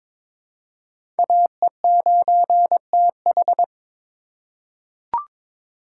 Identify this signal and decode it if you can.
Morse code, heard as a keyed tone, about 22 wpm, 700 Hz. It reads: AE9TH
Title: USA Call Signs - 5 Characters - 3x Character Spacing - 22wpm